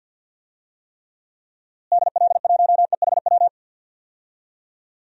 Morse code, heard as a keyed tone, about 52 wpm, 700 Hz. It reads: DL1EHW
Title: DXCC-100 Call Signs - 6 Characters - Repeated 2x - 50wpm